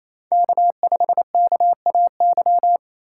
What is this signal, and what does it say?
Morse code, heard as a keyed tone, about 28 wpm, 700 Hz. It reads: K5KAY
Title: USA Call Signs - 5 Characters - 2x Character Spacing - 28wpm